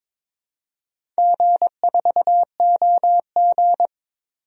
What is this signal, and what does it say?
Morse code, heard as a keyed tone, about 21 wpm, 700 Hz. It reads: G4OG
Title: DXCC-100 Call Signs - 4 Characters - 22wpm